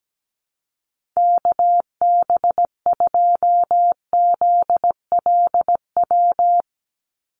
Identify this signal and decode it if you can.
Morse code, heard as a keyed tone, about 17 wpm, 700 Hz. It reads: KB2ZLW
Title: USA Call Signs 17wpm